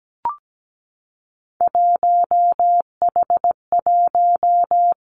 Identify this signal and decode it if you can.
Morse code, heard as a keyed tone, about 17 wpm, 700 Hz. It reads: 1H1
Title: Three Challenging Letter-Number Combinations 17wpm